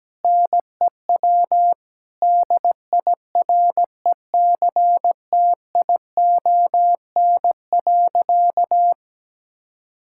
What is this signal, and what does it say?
Morse code, heard as a keyed tone, about 17 wpm, 700 Hz. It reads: NEW DIRECTION.
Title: Sentences from Top 1000 Words - Review Part 5 - 17wpm